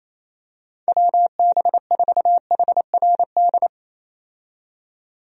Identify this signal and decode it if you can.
Morse code, heard as a keyed tone, about 28 wpm, 700 Hz. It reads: WB4HRD
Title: USA Call Signs - 6 Characters - 28wpm